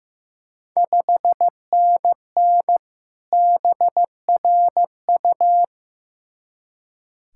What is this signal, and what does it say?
Morse code, heard as a keyed tone, about 15 wpm, 700 Hz. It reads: 5NN BRU
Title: Ontario QSO Party Exchange 15wpm